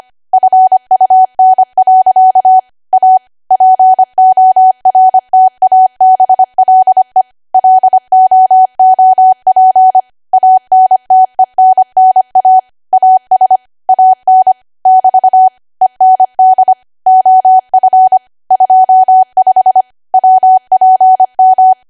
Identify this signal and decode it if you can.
Morse code, heard as a keyed tone, about 25 wpm, 750 Hz. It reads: FUN. A PORTABLE LOOP ANTENNA AS AN <BT> END OF 25 WPM